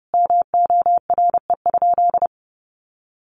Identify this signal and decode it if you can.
Morse code, heard as a keyed tone, about 28 wpm, 700 Hz. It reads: MORE?
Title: Sentences from Top 700 Words 30wpm